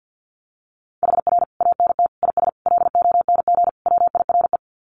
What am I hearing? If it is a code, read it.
Morse code, heard as a keyed tone, about 50 wpm, 700 Hz. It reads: HR ANT IS LONG WIRE
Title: Common QSO Phrases 50wpm